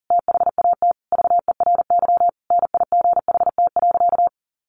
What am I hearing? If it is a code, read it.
Morse code, heard as a keyed tone, about 40 wpm, 700 Hz. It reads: THAT VERY NIGHT.
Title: Sentences from Top 1200 Words 40wpm